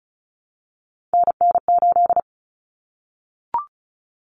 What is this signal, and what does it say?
Morse code, heard as a keyed tone, about 34 wpm, 700 Hz. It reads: NN8
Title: Common Cut Numbers - 3-Digit 35wpm